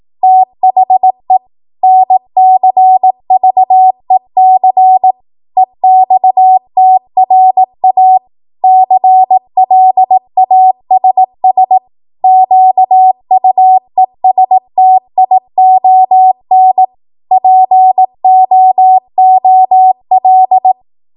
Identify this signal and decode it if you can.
Morse code, heard as a keyed tone, about 18 wpm, 750 Hz. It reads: THE NCVEC EXTRA CLASS QUESTION POOL